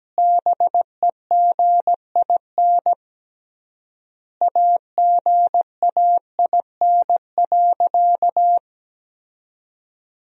Morse code, heard as a keyed tone, about 17 wpm, 700 Hz. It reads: BEGIN AGAIN.
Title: 3x Word-Spacing - Sentences from Top 300 Words - 17wpm